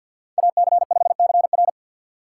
Morse code, heard as a keyed tone, about 52 wpm, 700 Hz. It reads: AC5CR
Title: USA Call Signs - 5 Characters - 6x Character Spacing - 50wpm